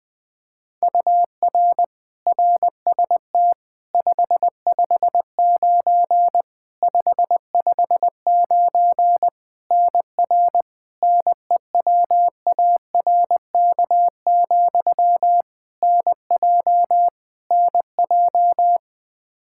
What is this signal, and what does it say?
Morse code, heard as a keyed tone, about 20 wpm, 700 Hz. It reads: UR RST 559 559 NR NEWARK, NJ NJ
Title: K1USN SST QSO Practice 20wpm